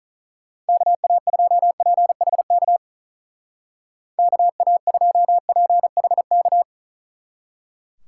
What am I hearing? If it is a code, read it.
Morse code, heard as a keyed tone, about 43 wpm, 700 Hz. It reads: KA2PHK KA2PHK
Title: Speed Racing - USA 2x3 Call Signs 35wpm